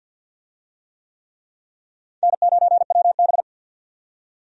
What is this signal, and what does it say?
Morse code, heard as a keyed tone, about 50 wpm, 700 Hz. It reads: N9WB
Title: USA Call Signs - 4 Characters - 50wpm